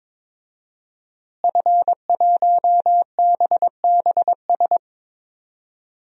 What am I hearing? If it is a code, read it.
Morse code, heard as a keyed tone, about 22 wpm, 700 Hz. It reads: F1BBS
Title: DXCC-100 Call Signs - 5 Characters - Repeated 3x - 22wpm